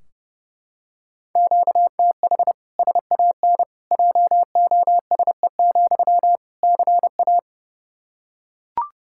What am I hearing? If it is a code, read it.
Morse code, heard as a keyed tone, about 32 wpm, 700 Hz. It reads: QTH SAN JOSE, CA
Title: Practice QSO - Exchange 1 - Beginning - 30wpm